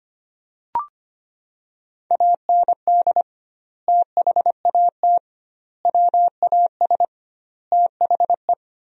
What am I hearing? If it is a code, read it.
Morse code, heard as a keyed tone, about 25 wpm, 700 Hz. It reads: AND THAT WAS THE
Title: 2x Word-Spacing - Sentences from Top 200 Words - 25wpm